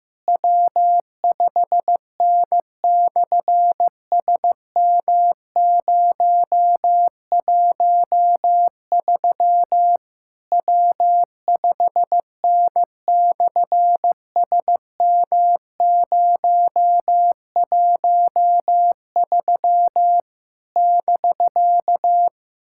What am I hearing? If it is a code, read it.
Morse code, heard as a keyed tone, about 15 wpm, 700 Hz. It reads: W5N/SM013 W5N/SM013 <BK>
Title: SOTA QSO Practice 15wpm